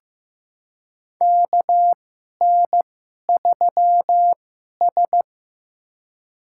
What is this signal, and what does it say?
Morse code, heard as a keyed tone, about 15 wpm, spaced as for 11 wpm, 700 Hz. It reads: KN3S
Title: USA Call Signs - 2x Character Spacing - 15wpm